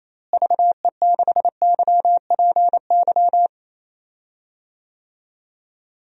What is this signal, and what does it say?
Morse code, heard as a keyed tone, about 28 wpm, 700 Hz. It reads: VE6YPY
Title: DXCC-100 Call Signs - 6 Characters - Repeated 2x - 28wpm